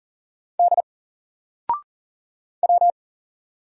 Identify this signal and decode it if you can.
Morse code, heard as a keyed tone, about 40 wpm, 700 Hz. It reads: D W
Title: Single Letter-Number 40wpm